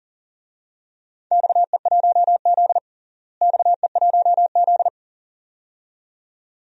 Morse code, heard as a keyed tone, about 40 wpm, 700 Hz. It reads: XE1Z XE1Z
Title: DXCC-100 Call Signs - 4 Characters - Repeated 2x - 40wpm